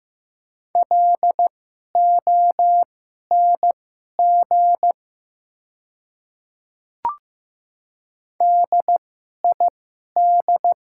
Morse code, heard as a keyed tone, about 15 wpm, spaced as for 11 wpm, 700 Hz. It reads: LONG DID
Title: 2x Character Spacing - Top 100 Words - 15wpm